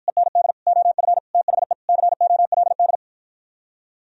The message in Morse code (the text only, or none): END OF THE COLD